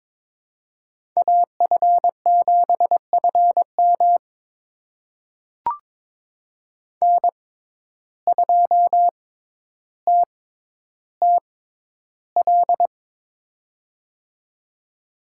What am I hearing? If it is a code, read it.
Morse code, heard as a keyed tone, about 22 wpm, spaced as for 11 wpm, 700 Hz. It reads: AF7FM N2TTL
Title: USA Call Signs - 5 Characters - 6x Character Spacing - 22wpm